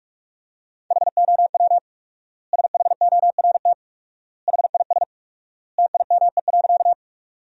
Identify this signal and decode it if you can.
Morse code, heard as a keyed tone, about 41 wpm, 700 Hz. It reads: HOW SHORT HIS TIME.
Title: 4x Word-Spacing - Sentences from Top 400 Words - 45wpm